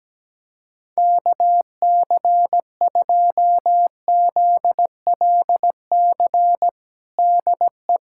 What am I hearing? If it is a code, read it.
Morse code, heard as a keyed tone, about 17 wpm, 700 Hz. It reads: KC2ZLC DE
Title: USA Callsign DE Callsign 17wpm